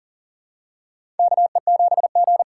Morse code, heard as a keyed tone, about 40 wpm, 700 Hz. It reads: KE7G
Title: USA Call Signs - 4 Characters - 4x Character Spacing - 40wpm